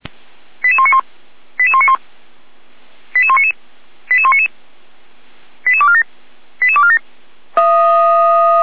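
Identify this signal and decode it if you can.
Five-tone selective call (ZVEI), heard as quick runs of five tones, tones of about 70 ms. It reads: 88191 88191 88199 88199 88247 88247
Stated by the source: Five-tone ZVEI selective call received on an SDR, from the Feuerwehr Orlamünde server.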